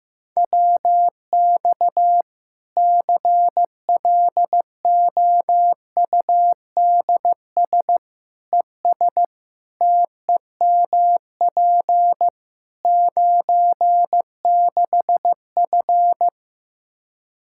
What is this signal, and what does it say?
Morse code, heard as a keyed tone, about 15 wpm, 700 Hz. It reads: WX CLOUDS ES TEMP 96F
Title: Practice QSO - Exchange 2 - Info - 15wpm